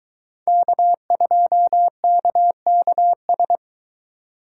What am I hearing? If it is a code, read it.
Morse code, heard as a keyed tone, about 20 wpm, 700 Hz. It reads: K2KKS